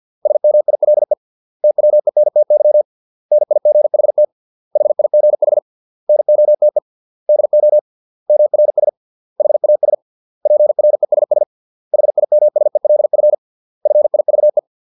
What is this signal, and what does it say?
Morse code, heard as a keyed tone, about 50 wpm, 600 Hz. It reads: SMILE TWENTY NIGHT HIGH NOTE DO MRS HAS PRESS HIMSELF FIVE